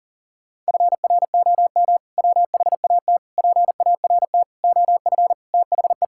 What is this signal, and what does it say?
Morse code, heard as a keyed tone, about 40 wpm, 700 Hz. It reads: FROM WHAT PART OF THE
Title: Sentences from Top 1000 Words - Review Part 6 - 40wpm